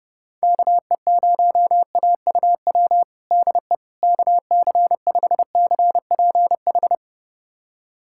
Morse code, heard as a keyed tone, about 30 wpm, 700 Hz. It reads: KE0AUW DE KC5CPH
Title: USA Callsign DE Callsign - 6 Characters - 30wpm